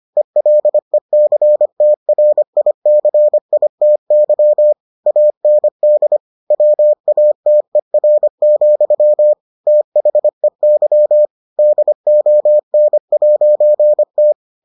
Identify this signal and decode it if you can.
Morse code, heard as a keyed tone, about 25 wpm, 600 Hz. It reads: ELECTRICITY AND WATER, THEY DON'T